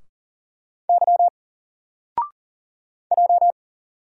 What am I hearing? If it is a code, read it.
Morse code, heard as a keyed tone, about 41 wpm, 700 Hz. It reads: Y J
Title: Single Character (Letter:Number:Punctuation:Prosign) Alt 40wpm